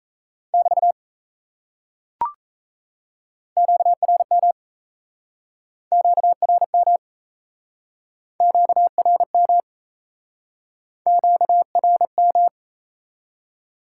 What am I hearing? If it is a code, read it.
Morse code, heard as a keyed tone, about 37 wpm, 700 Hz. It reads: <BT> QRM QRM QRM QRM